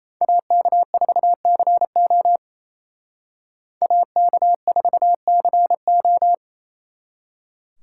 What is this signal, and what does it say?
Morse code, heard as a keyed tone, about 35 wpm, 700 Hz. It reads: AK4CO AK4CO
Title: Speed Racing - USA 2x2 Call Signs 28wpm